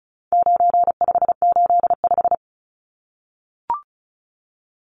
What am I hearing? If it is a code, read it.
Morse code, heard as a keyed tone, about 34 wpm, 700 Hz. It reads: 9585
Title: Numbers - 4-Digit 35wpm